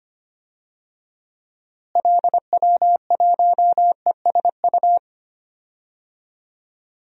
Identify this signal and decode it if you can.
Morse code, heard as a keyed tone, about 25 wpm, 700 Hz. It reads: LW1ESU
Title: DXCC-100 Call Signs - 6 Characters - Repeated 3x - 25wpm